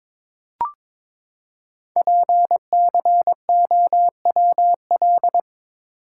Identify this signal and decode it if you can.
Morse code, heard as a keyed tone, about 22 wpm, 700 Hz. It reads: PCOWL
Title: Code Groups - 5 Letters - 22wpm